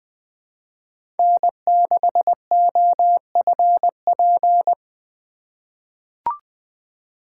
Morse code, heard as a keyed tone, about 20 wpm, 700 Hz. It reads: N6OFP
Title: USA Call Signs - 5 Characters - 2x Character Spacing - 20wpm